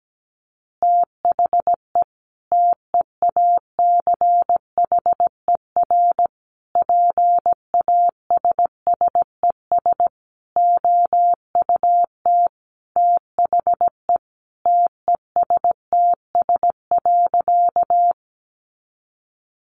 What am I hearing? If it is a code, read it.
Morse code, heard as a keyed tone, about 17 wpm, 700 Hz. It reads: THE TEACHER PASSES OUT THE TESTS.
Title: Additional Easy Phrases 17wpm